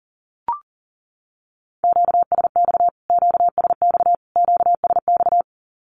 Morse code, heard as a keyed tone, about 40 wpm, 700 Hz. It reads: QSX QSX QSX
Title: Q-Codes Repeated 3x 40wpm